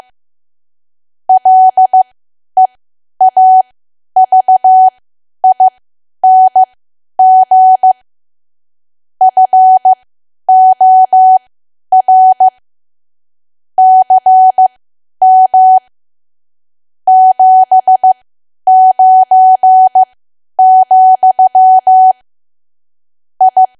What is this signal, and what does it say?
Morse code, heard as a keyed tone, about 15 wpm, spaced as for 10 wpm, 750 Hz. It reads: LEAVING FOR CM 79, I
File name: Archive / 10 WPM / 10WPM